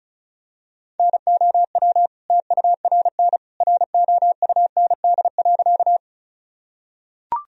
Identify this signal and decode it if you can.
Morse code, heard as a keyed tone, about 36 wpm, 700 Hz. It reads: NOW TURN ROUND.
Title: Sentences from Top 1000 Words - Review Part 1 - 35wpm